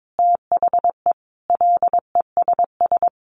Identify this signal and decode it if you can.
Morse code, heard as a keyed tone, about 22 wpm, 700 Hz. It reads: THE LESS
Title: Sentences from Top 1300 Words 22wpm